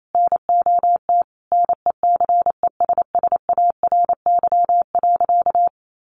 Morse code, heard as a keyed tone, about 28 wpm, 700 Hz. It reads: NOT NECESSARY.
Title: Sentences from Top 800 Words 28wpm